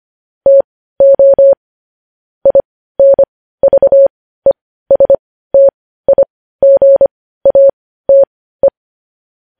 Morse code, heard as a keyed tone, about 25 wpm, spaced as for 15 wpm, 550 Hz. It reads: TO INVESTIGATE
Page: news headlines in morse code at 15 wpm